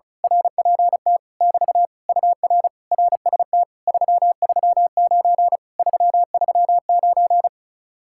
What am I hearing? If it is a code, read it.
Morse code, heard as a keyed tone, about 33 wpm, 700 Hz. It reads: RPT <BT> UR RST 339 339
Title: Speed-Racing Common QSO Phrases Encore - 35wpm